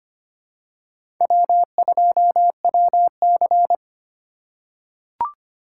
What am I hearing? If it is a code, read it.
Morse code, heard as a keyed tone, about 25 wpm, 700 Hz. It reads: W2WC
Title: USA Call Signs - 4 Characters - 4x Character Spacing - 25wpm